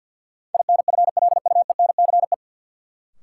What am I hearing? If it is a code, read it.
Morse code, heard as a keyed tone, about 50 wpm, 700 Hz. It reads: INFLUENCE
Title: Sentences from Top 1100 Words - New Word Drill - 50wpm